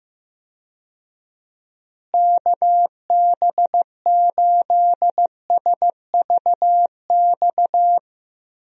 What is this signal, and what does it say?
Morse code, heard as a keyed tone, about 15 wpm, 700 Hz. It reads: KB8SVX
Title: USA Call Signs - 6 Characters - Repeated 2x - 15wpm